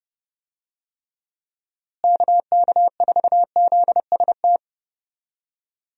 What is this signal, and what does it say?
Morse code, heard as a keyed tone, about 32 wpm, 700 Hz. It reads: KK4ZST
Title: USA Call Signs - 6 Characters - Repeated 3x - 30wpm